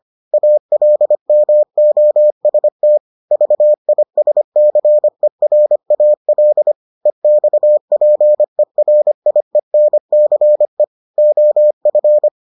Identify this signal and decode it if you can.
Morse code, heard as a keyed tone, about 25 wpm, 600 Hz. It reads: ALMOST VISCERAL EXPERIENCE OF